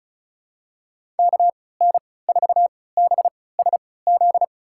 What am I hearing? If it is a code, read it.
Morse code, heard as a keyed tone, about 36 wpm, spaced as for 20 wpm, 700 Hz. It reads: KN4BSZ
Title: USA Call Signs - 6 Characters - 3x Character Spacing - 35wpm